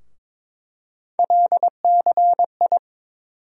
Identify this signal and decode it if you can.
Morse code, heard as a keyed tone, about 22 wpm, 700 Hz. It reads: LCI
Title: USA Call Sign Postfixes - 3 Letters 22wpm